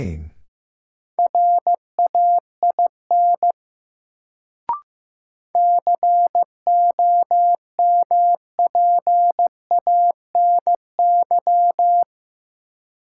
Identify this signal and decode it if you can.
Morse code, heard as a keyed tone, about 15 wpm, 700 Hz. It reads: RAIN COMPANY
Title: Sentences from Top 600 Words - New Word Drill - 15wpm